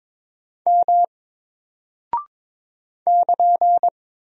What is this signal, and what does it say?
Morse code, heard as a keyed tone, about 22 wpm, 700 Hz. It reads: M <KN>
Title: Single Character (Letter:Number:Punctuation:Prosign) 22wpm